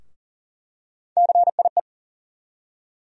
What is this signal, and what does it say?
Morse code, heard as a keyed tone, about 40 wpm, 700 Hz. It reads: CIE
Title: USA Call Sign Postfixes - 3 Letters 40wpm